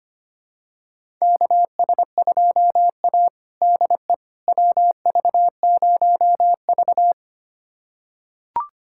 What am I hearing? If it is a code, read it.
Morse code, heard as a keyed tone, about 25 wpm, 700 Hz. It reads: KS2A DE WV0V
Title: USA Callsign DE Callsign - 4 Characters - 25wpm